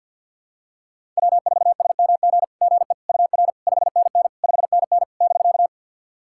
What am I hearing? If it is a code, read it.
Morse code, heard as a keyed tone, about 50 wpm, 700 Hz. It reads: W4SMG GE UR 5NN 5NN <BK>
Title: SOTA QSO Practice 50wpm